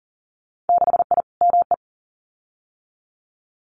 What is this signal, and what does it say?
Morse code, heard as a keyed tone, about 40 wpm, 700 Hz. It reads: 6I ME
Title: Winter Field Day Exchanges Encore 40wpm